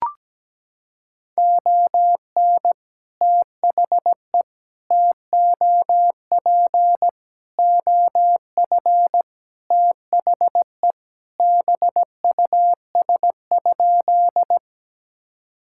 Morse code, heard as a keyed tone, about 17 wpm, 700 Hz. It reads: ON THE TOP OF THE BUS?
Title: Sentences from Top 1000 Words - Review Part 6 - 17wpm